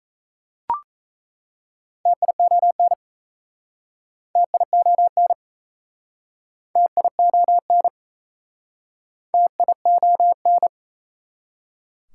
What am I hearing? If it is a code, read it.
Morse code, heard as a keyed tone, about 37 wpm, 700 Hz. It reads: TION TION TION TION